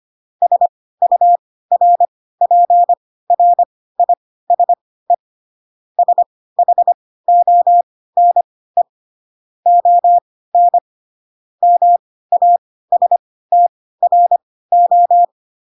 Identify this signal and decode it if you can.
Morse code, heard as a keyed tone, about 25 wpm, spaced as for 16 wpm, 700 Hz. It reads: SURPRISE SHONE ON MASTRO
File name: Pinocchio / Pinocchio ch1 10 25 ik1hge